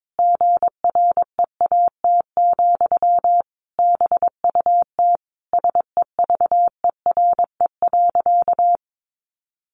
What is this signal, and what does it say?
Morse code, heard as a keyed tone, about 22 wpm, 700 Hz. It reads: GREAT, BUT SEVERE.
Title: Sentences from Top 1400 Words 22wpm